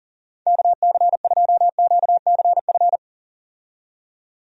Morse code, heard as a keyed tone, about 40 wpm, 700 Hz. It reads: KC2QCF
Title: USA Call Signs - 6 Characters - 40wpm